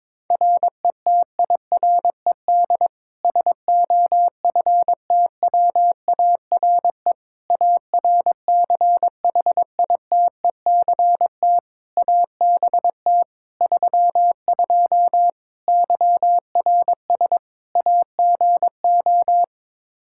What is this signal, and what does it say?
Morse code, heard as a keyed tone, about 22 wpm, 700 Hz. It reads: RETIRED SOFTWARE ARCHITECT ABT 32 YRS AGO
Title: Practice QSO - Exchange 3 - More Info - 22wpm